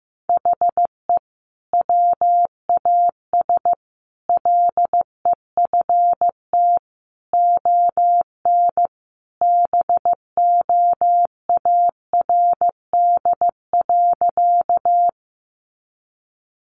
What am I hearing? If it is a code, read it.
Morse code, heard as a keyed tone, about 15 wpm, 700 Hz. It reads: HE WAS LEFT ON BOARD.
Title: Sentences from Top 700 Words 15wpm